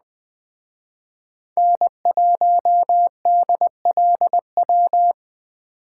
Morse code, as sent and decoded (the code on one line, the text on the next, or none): -. .---- -.. .-.. .--
N1DLW